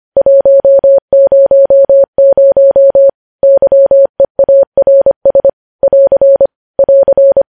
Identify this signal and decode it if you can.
Morse code, heard as a keyed tone, about 25 wpm, 550 Hz. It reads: 100 YEARS <AR> <AR>
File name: podcast / News headlines-1x-25x25